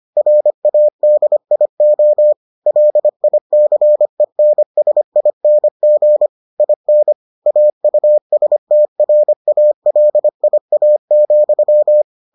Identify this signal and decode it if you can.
Morse code, heard as a keyed tone, about 25 wpm, 600 Hz. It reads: RADIO LICENSING IN AUSTRALIA,